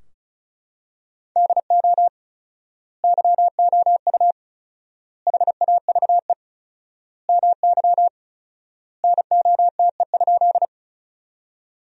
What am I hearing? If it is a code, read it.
Morse code, heard as a keyed tone, about 34 wpm, 700 Hz. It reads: DO YOU HAVE MY NOTE?
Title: 4x Word-Spacing - Sentences from Top 500 Words - 35wpm